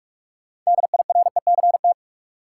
Morse code, heard as a keyed tone, about 42 wpm, 700 Hz. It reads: DIRECT
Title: Sentences from Top 400 Words - New Word Drill - 45wpm